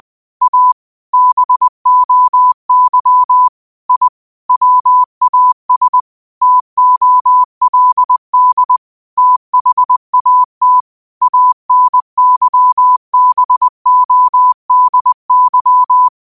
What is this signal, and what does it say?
Morse code, heard as a keyed tone, about 20 wpm, 1000 Hz. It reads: A BOY I WAS TOLD THAT ANYBODY
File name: qotdfiles / 20wpm